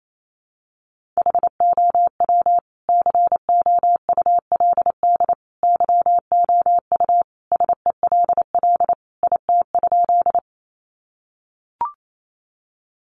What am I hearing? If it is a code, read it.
Morse code, heard as a keyed tone, about 28 wpm, 700 Hz. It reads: HOW COULD YOU SELL IT?
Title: Sentences from Top 1100 Words 28wpm